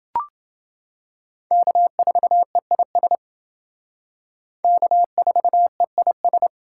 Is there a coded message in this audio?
Yes: Morse code from 1.5 to 6.5 s, heard as a keyed tone, about 28 wpm, 700 Hz. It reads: K4EIS K4EIS